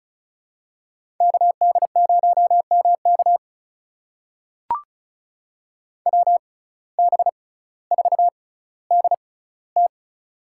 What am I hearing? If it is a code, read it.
Morse code, heard as a keyed tone, about 34 wpm, spaced as for 18 wpm, 700 Hz. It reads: KD0MK WB4DT
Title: USA Call Signs - 5 Characters - 6x Character Spacing - 35wpm